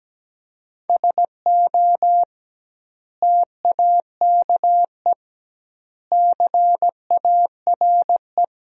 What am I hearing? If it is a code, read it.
Morse code, heard as a keyed tone, about 17 wpm, 700 Hz. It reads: SO TAKE CARE
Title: 2x Word-Spacing - Sentences from Top 300 Words - 17wpm